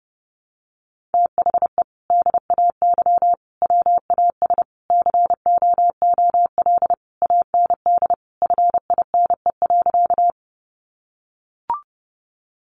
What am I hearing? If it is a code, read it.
Morse code, heard as a keyed tone, about 28 wpm, 700 Hz. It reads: THE DAY WAS COOL AND FINE.
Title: Sentences from Top 1200 Words 30wpm